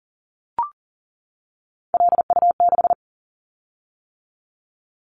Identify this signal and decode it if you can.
Morse code, heard as a keyed tone, about 40 wpm, 700 Hz. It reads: LU6
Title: Callsign Prefixes for Top 100 DXCC - 40wpm